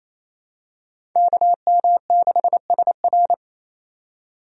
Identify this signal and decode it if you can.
Morse code, heard as a keyed tone, about 28 wpm, 700 Hz. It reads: KM6SR
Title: USA Call Signs - 5 Characters - 28wpm